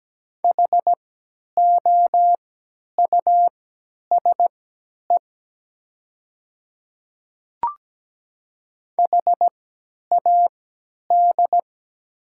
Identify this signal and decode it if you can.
Morse code, heard as a keyed tone, about 17 wpm, spaced as for 10 wpm, 700 Hz. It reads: HOUSE HAD
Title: 3x Character Spacing - Top 300 Words - 17wpm